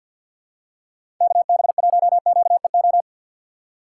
Morse code, heard as a keyed tone, about 51 wpm, 700 Hz. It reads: KB1QEO
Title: USA Call Signs - 3x Character Spacing - 50wpm